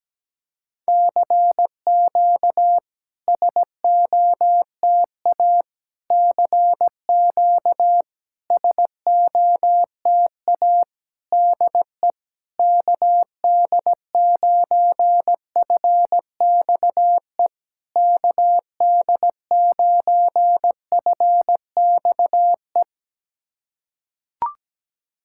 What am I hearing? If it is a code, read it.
Morse code, heard as a keyed tone, about 17 wpm, 700 Hz. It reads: CQ SOTA CQ SOTA DE KD9FXE KD9FXE
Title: SOTA QSO Practice 17wpm